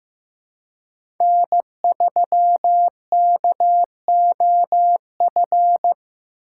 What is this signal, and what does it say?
Morse code, heard as a keyed tone, about 18 wpm, 700 Hz. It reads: N3KOF
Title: Speed Racing - USA 1x3 Call Signs 15wpm